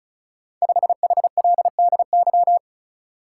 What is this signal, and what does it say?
Morse code, heard as a keyed tone, about 36 wpm, 690 Hz. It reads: 5HLDY
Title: Code Groups - 5 Letter:Numbers - 35wpm